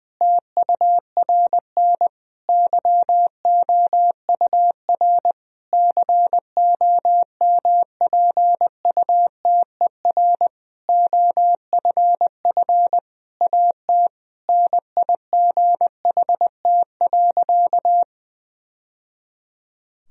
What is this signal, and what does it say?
Morse code, heard as a keyed tone, about 21 wpm, 700 Hz. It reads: TURN YOUR COMPUTER OFF AT NIGHT.
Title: Sentences from Top 1000 Words - Review Part 3 - 20wpm